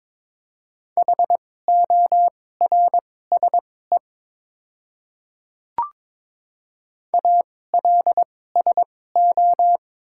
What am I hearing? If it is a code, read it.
Morse code, heard as a keyed tone, about 22 wpm, spaced as for 16 wpm, 700 Hz. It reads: HORSE ALSO
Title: 2x Character Spacing - Top 500 Words - 22wpm